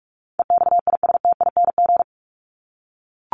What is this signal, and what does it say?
Morse code, heard as a keyed tone, about 44 wpm, 700 Hz. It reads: EXISTING
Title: Top 1251-1500 Words 45wpm